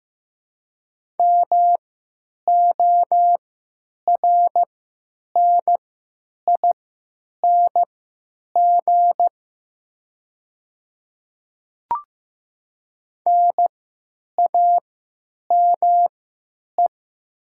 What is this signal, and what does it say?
Morse code, heard as a keyed tone, about 15 wpm, spaced as for 9 wpm, 700 Hz. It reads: MORNING NAME
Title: 3x Character Spacing - Top 500 Words - 15wpm